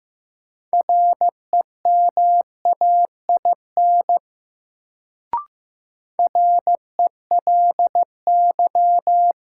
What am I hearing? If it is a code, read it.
Morse code, heard as a keyed tone, about 15 wpm, 700 Hz. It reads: REMAIN RELY
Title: Sentences from Top 1200 Words - New Word Drill - 15wpm